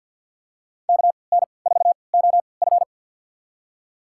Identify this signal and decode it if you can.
Morse code, heard as a keyed tone, about 51 wpm, spaced as for 29 wpm, 700 Hz. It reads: KN4OF
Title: USA Call Signs - 3x Character Spacing - 50wpm